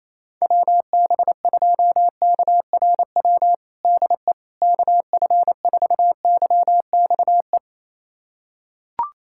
Morse code, heard as a keyed tone, about 28 wpm, 700 Hz. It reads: WB2KRW DE KF4YXE
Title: USA Callsign DE Callsign - 6 Characters - 28wpm